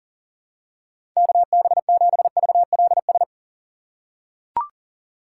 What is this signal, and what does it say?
Morse code, heard as a keyed tone, about 37 wpm, 700 Hz. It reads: KB7VLS
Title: USA Call Signs - 6 Characters - 3x Character Spacing - 40wpm